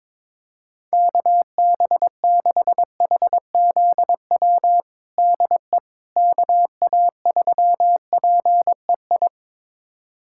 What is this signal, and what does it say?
Morse code, heard as a keyed tone, about 22 wpm, 700 Hz. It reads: KB6HZW DE KA3PEI
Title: USA Callsign DE Callsign - 6 Characters - 22wpm